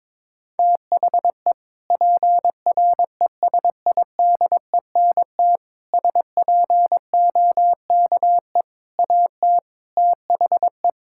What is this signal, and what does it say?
Morse code, heard as a keyed tone, about 23 wpm, 700 Hz. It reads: THE PRESIDENT SPOKE AT THE
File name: practice / 22wpm / Sentences with 4-Word Phrases - 22wpm